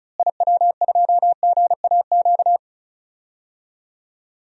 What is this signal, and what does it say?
Morse code, heard as a keyed tone, about 35 wpm, 700 Hz. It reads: IW2GAQ
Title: DXCC-100 Call Signs - 6 Characters - Repeated 3x - 35wpm